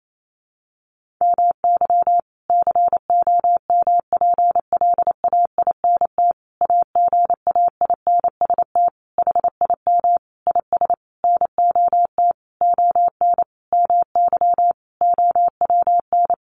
Morse code, heard as a keyed tone, about 28 wpm, 700 Hz. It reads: MY COMPLAINT AGAINST HIM IS NOT ON MY OWN
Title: Sentences from Top 1400 Words 28wpm